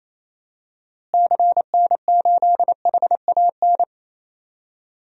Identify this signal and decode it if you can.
Morse code, heard as a keyed tone, about 28 wpm, 700 Hz. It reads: CN8HAN
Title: DXCC-100 Call Signs - 6 Characters - Repeated 2x - 28wpm